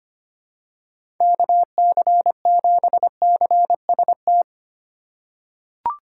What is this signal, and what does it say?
Morse code, heard as a keyed tone, about 25 wpm, 700 Hz. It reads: KC7CST
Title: USA Call Signs - 6 Characters - 2x Character Spacing - 25wpm